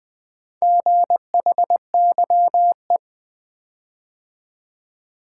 Morse code, heard as a keyed tone, about 20 wpm, 700 Hz. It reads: GHYE
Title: Four Letter-Number Combinations - 20wpm